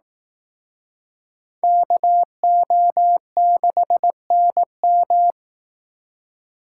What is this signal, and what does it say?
Morse code, heard as a keyed tone, about 18 wpm, 700 Hz. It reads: KO6NM
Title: Speed Racing - USA 2x2 Call Signs 15wpm